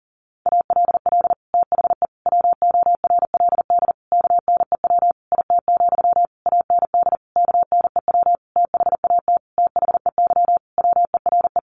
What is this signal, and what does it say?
Morse code, heard as a keyed tone, about 40 wpm, 700 Hz. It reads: ALL THE WORLD KNEW IT, AND KNEW THAT THEY WERE
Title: Sentences from Top 1200 Words 40wpm